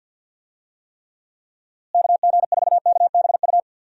Morse code, heard as a keyed tone, about 52 wpm, 700 Hz. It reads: KG4KBU
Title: USA Call Signs - 6 Characters - Repeated 2x - 50wpm